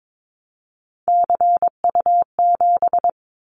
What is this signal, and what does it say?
Morse code, heard as a keyed tone, about 22 wpm, 700 Hz. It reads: CU7
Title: Callsign Prefixes for Top 50 DXCC - 22wpm